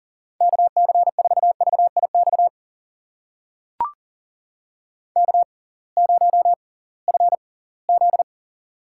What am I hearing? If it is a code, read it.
Morse code, heard as a keyed tone, about 39 wpm, spaced as for 23 wpm, 700 Hz. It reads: KC4VIX K0FZ
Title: USA Call Signs - 6x Character Spacing - 40wpm